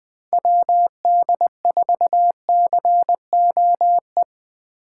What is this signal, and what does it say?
Morse code, heard as a keyed tone, about 19 wpm, 700 Hz. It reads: WD4COE
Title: Speed Racing - USA Call Signs 17wpm